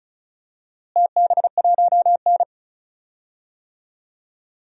Morse code, heard as a keyed tone, about 34 wpm, 700 Hz. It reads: TB1N
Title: DXCC-100 Call Signs - 4 Characters - Repeated 2x - 35wpm